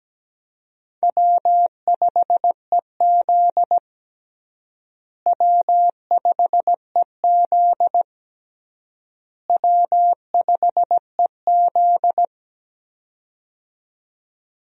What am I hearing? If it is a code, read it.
Morse code, heard as a keyed tone, about 17 wpm, 700 Hz. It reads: W5EZ W5EZ W5EZ
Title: USA Call Signs - 4 Characters - Repeated 3x - 17wpm